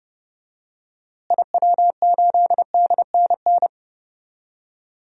Morse code, heard as a keyed tone, about 30 wpm, 700 Hz. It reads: IW8DNN
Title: DXCC-100 Call Signs - 6 Characters - Repeated 3x - 30wpm